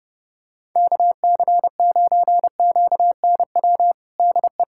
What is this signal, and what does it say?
Morse code, heard as a keyed tone, about 32 wpm, 700 Hz. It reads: KC9QNW DE
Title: USA Callsign DE Callsign 30wpm